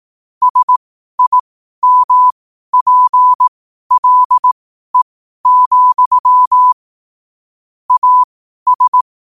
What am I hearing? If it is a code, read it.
Morse code, heard as a keyed tone, about 18 wpm, spaced as for 12 wpm, 1000 Hz. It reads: SIMPLE, AS